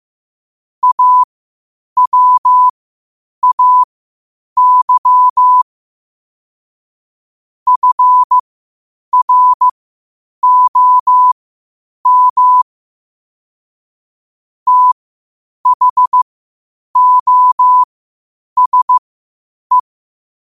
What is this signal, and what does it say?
Morse code, heard as a keyed tone, about 15 wpm, spaced as for 8 wpm, 1000 Hz. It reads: AWAY FROM THOSE
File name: qotdfiles / 8wpm